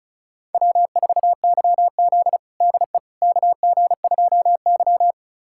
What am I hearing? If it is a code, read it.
Morse code, heard as a keyed tone, about 34 wpm, 690 Hz. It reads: W4YZ DE KG2Y
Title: USA Callsign DE Callsign - 4 Characters - 35wpm